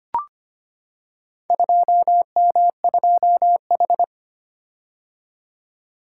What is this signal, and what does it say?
Morse code, heard as a keyed tone, about 25 wpm, 700 Hz. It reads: 2M2H
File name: practice / 25wpm / Four Letter-Number Combinations - 25wpm